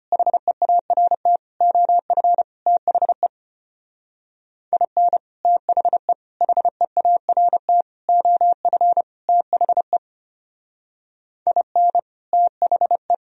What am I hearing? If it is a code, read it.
Morse code, heard as a keyed tone, about 32 wpm, 700 Hz. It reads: HEART OF THE IN THE HEART OF THE IN THE